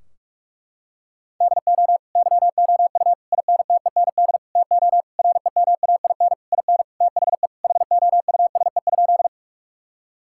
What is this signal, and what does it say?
Morse code, heard as a keyed tone, about 41 wpm, 700 Hz. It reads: DO YOU INTEND TO REMAIN IN THE HOUSE?
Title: Sentences from Top 1600 Words 45wpm